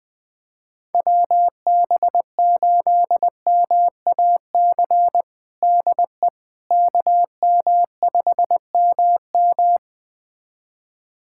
Morse code, heard as a keyed tone, about 20 wpm, 700 Hz. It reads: WB8MAC DE KM5MM
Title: USA Callsign DE Callsign 20wpm